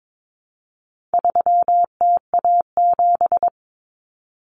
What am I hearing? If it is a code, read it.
Morse code, heard as a keyed tone, about 22 wpm, 700 Hz. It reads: 3TA7
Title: Common Cut Numbers - 4-Digit 22wpm